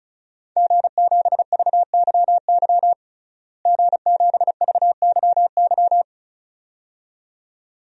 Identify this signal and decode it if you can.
Morse code, heard as a keyed tone, about 36 wpm, 700 Hz. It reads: G7VYY G7VYY
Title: DXCC-100 Call Signs - 5 Characters - Repeated 2x - 35wpm